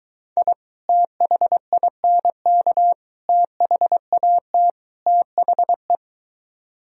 Morse code, heard as a keyed tone, about 23 wpm, 700 Hz. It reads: I THINK THAT THE